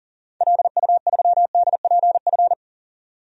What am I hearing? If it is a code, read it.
Morse code, heard as a keyed tone, about 37 wpm, 700 Hz. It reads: LU3DPF
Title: DXCC-100 Call Signs - 6 Characters - 40wpm